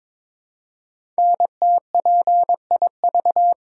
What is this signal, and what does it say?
Morse code, heard as a keyed tone, about 22 wpm, 700 Hz. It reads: NTPIV
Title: Code Groups - 5 Letter:Numbers - 22wpm